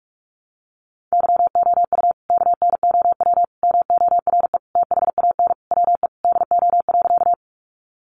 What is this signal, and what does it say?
Morse code, heard as a keyed tone, about 44 wpm, 700 Hz. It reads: YOU KNOW MORE THAN WE DO.
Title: Sentences from Top 100 Words 45wpm